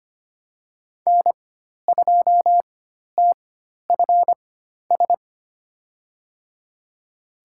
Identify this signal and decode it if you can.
Morse code, heard as a keyed tone, about 25 wpm, spaced as for 12 wpm, 700 Hz. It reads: N2TFS
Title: USA Call Signs - 5 Characters - 4x Character Spacing - 25wpm